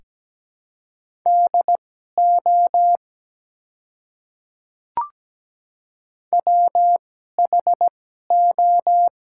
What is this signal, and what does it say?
Morse code, heard as a keyed tone, about 17 wpm, spaced as for 12 wpm, 700 Hz. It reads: DO WHO
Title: 2x Character Spacing - Top 100 Words - 17wpm